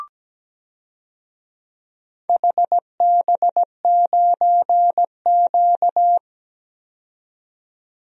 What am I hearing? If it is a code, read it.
Morse code, heard as a keyed tone, about 17 wpm, 700 Hz. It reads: HB9Q
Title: DXCC-100 Call Signs - 4 Characters - 17wpm